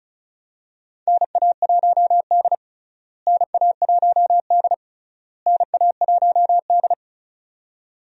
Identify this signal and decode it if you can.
Morse code, heard as a keyed tone, about 36 wpm, 700 Hz. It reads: NA1D NA1D NA1D
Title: USA Call Signs - 4 Characters - Repeated 3x - 35wpm